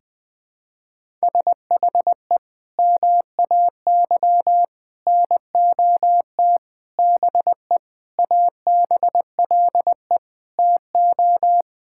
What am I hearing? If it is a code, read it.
Morse code, heard as a keyed tone, about 21 wpm, 700 Hz. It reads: SHE MAY NOT BE ABLE TO
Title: Sentences with 5-Word Phrases - 20wpm